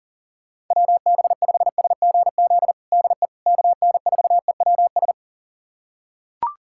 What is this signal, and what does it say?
Morse code, heard as a keyed tone, about 40 wpm, 700 Hz. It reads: WB5SGZ DE KN4EWS
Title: USA Callsign DE Callsign 40wpm